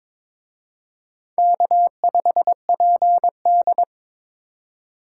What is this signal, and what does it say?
Morse code, heard as a keyed tone, about 25 wpm, 700 Hz. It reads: K5PD
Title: Speed Racing - USA Call Signs 22wpm